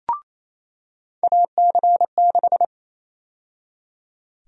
Morse code, heard as a keyed tone, about 28 wpm, 700 Hz. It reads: AC6
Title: USA Call Sign Prefixes - 28wpm